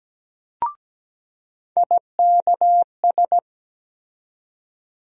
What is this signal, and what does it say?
Morse code, heard as a keyed tone, about 17 wpm, 700 Hz. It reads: IKS
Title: USA Call Sign Postfixes - 3 Letters 17wpm